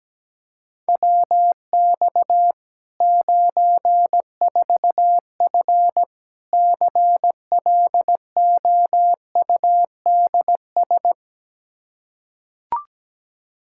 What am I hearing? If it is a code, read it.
Morse code, heard as a keyed tone, about 17 wpm, 700 Hz. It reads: WX 94F CLOUDS
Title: Practice QSO - Exchange 2 - Info - 17wpm